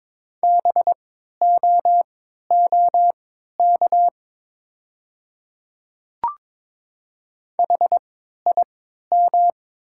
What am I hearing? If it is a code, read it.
Morse code, heard as a keyed tone, about 22 wpm, spaced as for 13 wpm, 700 Hz. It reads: BOOK HIM
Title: 3x Character Spacing - Top 300 Words - 22wpm